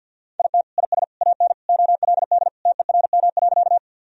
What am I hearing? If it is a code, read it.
Morse code, heard as a keyed tone, about 50 wpm, 700 Hz. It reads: IT IS AN OLD TERM.
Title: Sentences from Top 1000 Words - Review Part 4 - 50wpm